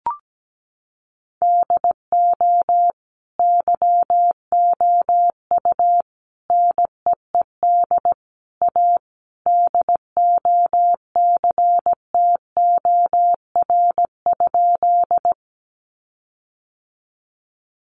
Morse code, heard as a keyed tone, about 17 wpm, 700 Hz. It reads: DO YOU NEED A DOCTOR?
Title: Sentences from Top 1500 Words 17wpm